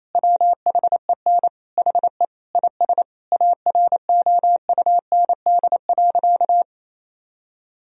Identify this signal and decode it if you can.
Morse code, heard as a keyed tone, about 27 wpm, 700 Hz. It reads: WHEN HE IS AROUND.
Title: Sentences from Top 1000 Words - Review Part 4 - 28wpm